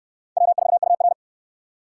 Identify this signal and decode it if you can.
Morse code, heard as a keyed tone, about 62 wpm, 700 Hz. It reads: W4UR